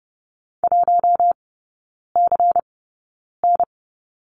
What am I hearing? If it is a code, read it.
Morse code, heard as a keyed tone, about 30 wpm, spaced as for 9 wpm, 700 Hz. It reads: 1CN
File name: Lesson 17e - Letter u - Three Characters 30wpm